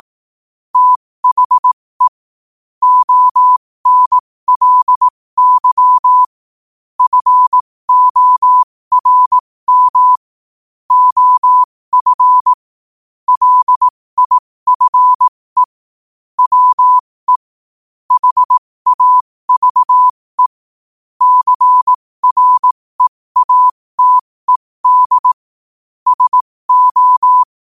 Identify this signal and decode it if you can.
Morse code, heard as a keyed tone, about 18 wpm, spaced as for 15 wpm, 1000 Hz. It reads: THE ONLY FORM OF LIFE WE HAVE CREATED SO
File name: qotdfiles / 15wpm